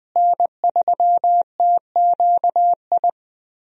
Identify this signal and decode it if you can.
Morse code, heard as a keyed tone, about 20 wpm, 700 Hz. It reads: N3TQI
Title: USA Call Signs - 5 Characters - Repeated 2x - 20wpm